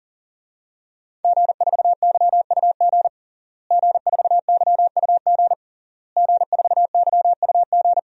Morse code, heard as a keyed tone, about 40 wpm, 700 Hz. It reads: G4YUG G4YUG G4YUG
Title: DXCC-100 Call Signs - 5 Characters - Repeated 3x - 40wpm